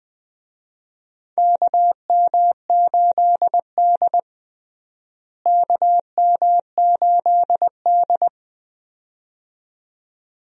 Morse code, heard as a keyed tone, about 20 wpm, 700 Hz. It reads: KM8D KM8D
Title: USA Call Signs - 4 Characters - Repeated 2x - 20wpm